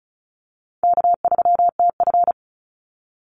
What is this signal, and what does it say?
Morse code, heard as a keyed tone, about 34 wpm, 700 Hz. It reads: K3TF
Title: Easy Call Signs Repeated 3x - 35wpm